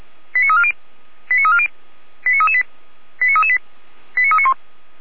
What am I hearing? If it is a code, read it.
Five-tone selective call (ZVEI), heard as quick runs of five tones, tones of about 70 ms. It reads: 89340 89340 89338 89338 89391